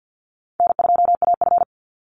Morse code, heard as a keyed tone, about 49 wpm, 700 Hz. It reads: N2AF
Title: NA SOTA Chaser Callsigns 50wpm